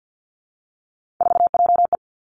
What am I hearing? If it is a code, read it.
Morse code, heard as a keyed tone, about 50 wpm, 700 Hz. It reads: 4JE
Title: Three Challenging Letter-Number Combinations 50wpm